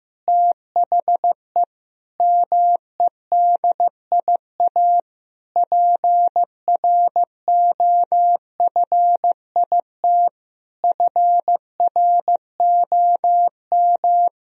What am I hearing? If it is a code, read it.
Morse code, heard as a keyed tone, about 15 wpm, 700 Hz. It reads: THE MEDIA PROFIT FROM
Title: Sentences from Top 1000 Words - Review Part 4 - 15wpm